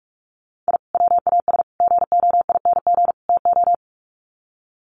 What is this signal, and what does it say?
Morse code, heard as a keyed tone, about 44 wpm, 700 Hz. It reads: I WAS GOING TO
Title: Sets of 4 Words 45wpm